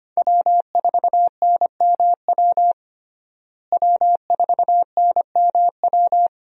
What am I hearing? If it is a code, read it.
Morse code, heard as a keyed tone, about 25 wpm, 700 Hz. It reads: W4NMW W4NMW
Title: USA Call Signs - 5 Characters - Repeated 3x - 25wpm